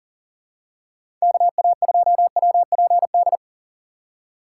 Morse code, heard as a keyed tone, about 40 wpm, 700 Hz. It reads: KA2WPD
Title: USA Call Signs - 2x Character Spacing - 40wpm